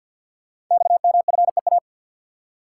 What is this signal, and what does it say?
Morse code, heard as a keyed tone, about 51 wpm, 700 Hz. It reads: XMFEA
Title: Code Groups - 5 Letter:Numbers - 50wpm